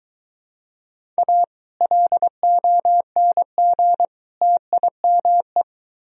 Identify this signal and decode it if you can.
Morse code, heard as a keyed tone, about 23 wpm, 700 Hz. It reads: A LONG TIME